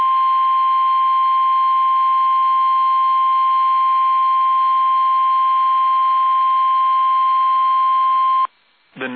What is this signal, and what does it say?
Emergency Alert System (SAME) tones (the 1050 Hz attention tone). It is heard from 0.7 to 8.2 s.
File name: eas / KUSO / KUSO